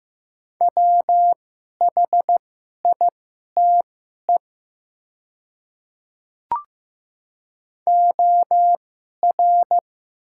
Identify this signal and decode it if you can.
Morse code, heard as a keyed tone, about 15 wpm, spaced as for 11 wpm, 700 Hz. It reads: WHITE OR
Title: 2x Character Spacing - Top 300 Words - 15wpm